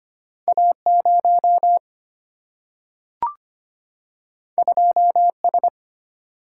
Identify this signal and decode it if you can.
Morse code, heard as a keyed tone, about 25 wpm, 700 Hz. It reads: A 0 2 S
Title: Single-Letter Single-Number 25wpm